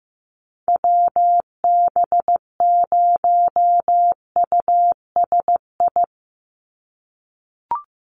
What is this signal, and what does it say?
Morse code, heard as a keyed tone, about 15 wpm, 700 Hz. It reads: WB0USI
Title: NA SOTA Activator Callsigns 15wpm